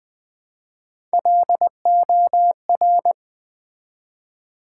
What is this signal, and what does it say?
Morse code, heard as a keyed tone, about 20 wpm, 700 Hz. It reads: LOR
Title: Build A Word - 1 Letter At A Time - 4-Letter Words - 20wpm